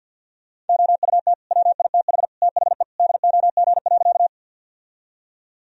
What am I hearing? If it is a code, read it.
Morse code, heard as a keyed tone, about 46 wpm, 700 Hz. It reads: OUT WITH THE DOG.
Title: Sentences from Top 1000 Words - Review Part 2 - 50wpm